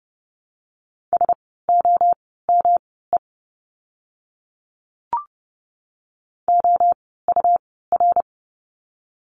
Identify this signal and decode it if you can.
Morse code, heard as a keyed tone, about 30 wpm, spaced as for 17 wpm, 700 Hz. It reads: SOME OUR
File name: course / 30:10wpm / 115 Lesson 19f - Letter w - Words 30wpm